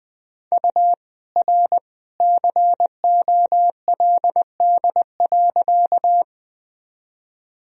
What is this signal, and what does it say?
Morse code, heard as a keyed tone, about 20 wpm, 700 Hz. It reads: U R COLD.
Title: Sentences from Top 400 Words and Abbreviations 20wpm